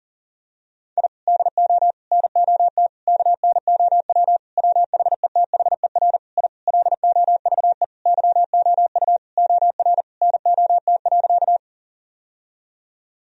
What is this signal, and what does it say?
Morse code, heard as a keyed tone, about 40 wpm, 700 Hz. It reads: I DO NOT KNOW WHETHER I LOVE YOU OR NOT.
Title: Sentences from Top 1000 Words - Review Part 3 - 40wpm